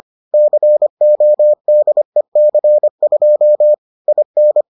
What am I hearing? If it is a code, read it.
Morse code, heard as a keyed tone, about 25 wpm, 600 Hz. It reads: CODEC2 IN